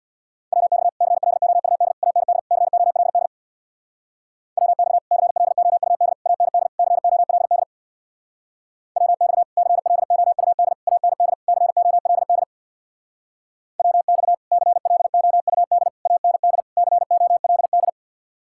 Morse code, heard as a keyed tone, about 67 wpm, 700 Hz. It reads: WX CLOUD AND COLD WX CLOUD AND COLD WX CLOUD AND COLD WX CLOUD AND COLD